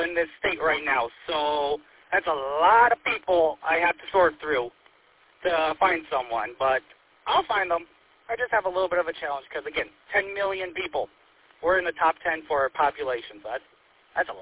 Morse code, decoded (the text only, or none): none